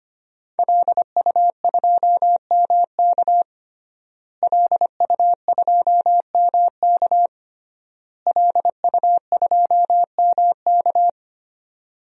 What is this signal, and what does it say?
Morse code, heard as a keyed tone, about 25 wpm, 700 Hz. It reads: LU2MK LU2MK LU2MK
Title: DXCC-100 Call Signs - 5 Characters - Repeated 3x - 25wpm